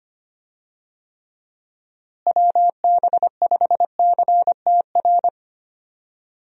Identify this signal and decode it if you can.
Morse code, heard as a keyed tone, about 25 wpm, 700 Hz. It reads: WB5CTR
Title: USA Call Signs - 6 Characters - Repeated 2x - 25wpm